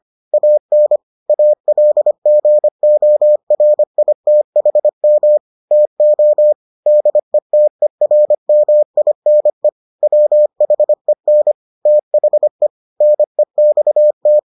Morse code, heard as a keyed tone, about 25 wpm, 600 Hz. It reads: AN ALGORITHM TO DETERMINE WHEN THE NEXT